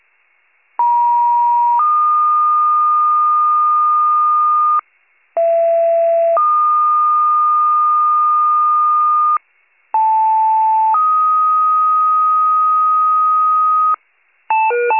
Two-tone page, heard as a one-second tone, then a three-second tone, about 950 then 1220 Hz, about 670 then 1220 Hz, about 880 then 1250 Hz. It is heard from 0.8 to 13.9 s.